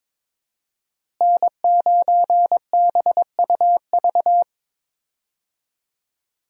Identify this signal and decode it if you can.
Morse code, heard as a keyed tone, about 22 wpm, 700 Hz. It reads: N9BUV
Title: USA Call Signs - 5 Characters - Repeated 3x - 22wpm